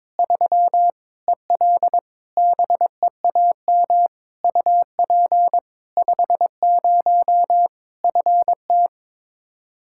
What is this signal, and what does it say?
Morse code, heard as a keyed tone, about 22 wpm, 700 Hz. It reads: 3 EL BEAM UP 50 FT
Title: Practice QSO - Exchange 2 - Info - 22wpm